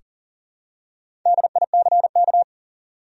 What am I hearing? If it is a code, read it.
Morse code, heard as a keyed tone, about 40 wpm, 700 Hz. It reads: DICK
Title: USA Common Names - 40wpm